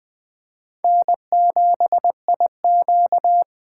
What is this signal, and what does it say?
Morse code, heard as a keyed tone, about 20 wpm, 700 Hz. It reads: N7IQ
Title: USA Call Signs - 3x Character Spacing - 20wpm